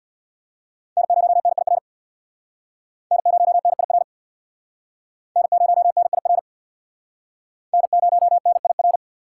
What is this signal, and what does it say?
Morse code, heard as a keyed tone, about 69 wpm, 700 Hz. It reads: N0NIR N0NIR N0NIR N0NIR